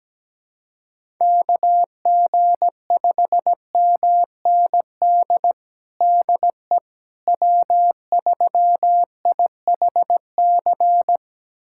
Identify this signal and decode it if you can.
Morse code, heard as a keyed tone, about 17 wpm, 700 Hz. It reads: KG5MND DE W3IHC
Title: USA Callsign DE Callsign 17wpm